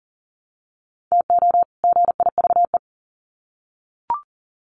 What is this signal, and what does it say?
Morse code, heard as a keyed tone, about 40 wpm, 700 Hz. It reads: TO GIVE
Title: Sets of 2 Words 40wpm